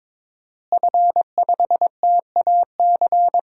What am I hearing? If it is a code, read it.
Morse code, heard as a keyed tone, about 22 wpm, 700 Hz. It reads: F5TAC
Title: DXCC-100 Call Signs - 5 Characters - Repeated 2x - 22wpm